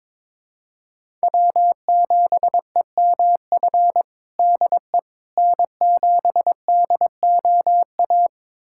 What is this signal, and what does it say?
Morse code, heard as a keyed tone, about 22 wpm, 700 Hz. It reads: W7EMF DE N7DOA
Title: USA Callsign DE Callsign - 5 Characters - 22wpm